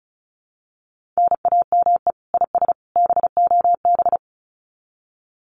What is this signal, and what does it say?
Morse code, heard as a keyed tone, about 34 wpm, 700 Hz. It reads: NAME IS BOB
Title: Common QSO Phrases 35wpm